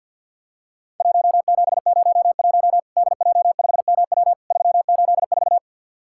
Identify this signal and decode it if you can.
Morse code, heard as a keyed tone, about 50 wpm, 700 Hz. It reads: 1701 DJ5MW 384
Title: Worked All Europe DX Contest - QTC Exchange - 50wpm